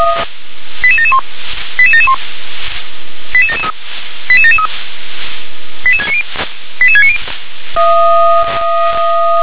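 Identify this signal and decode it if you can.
Five-tone selective call (ZVEI), heard as quick runs of five tones, tones of about 70 ms. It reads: 88771 88771 88773 88773 88690 88690
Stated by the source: ZVEI five-tone selective call received with an SDR.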